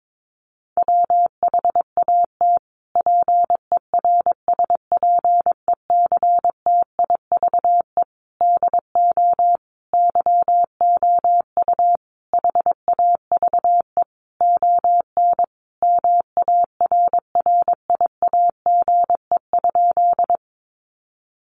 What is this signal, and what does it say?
Morse code, heard as a keyed tone, about 22 wpm, 700 Hz. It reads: WHAT PERSPECTIVE DO YOU HAVE ON MARRIAGE?
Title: Sentences from Top 1200 Words 22wpm